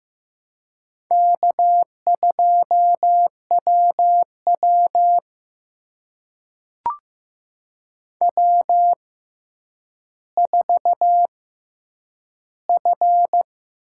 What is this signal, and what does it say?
Morse code, heard as a keyed tone, about 15 wpm, spaced as for 9 wpm, 700 Hz. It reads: K2WW W4F
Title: USA Call Signs - 4 Characters - 6x Character Spacing - 15wpm